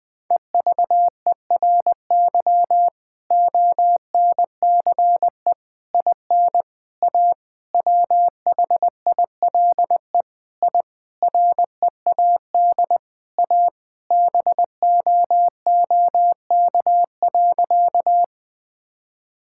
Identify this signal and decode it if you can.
Morse code, heard as a keyed tone, about 21 wpm, 700 Hz. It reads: EVERY ONCE IN A WHILE I READ A BOOK.
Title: Sentences with 5-Word Phrases - Spoken Beginning - 20wpm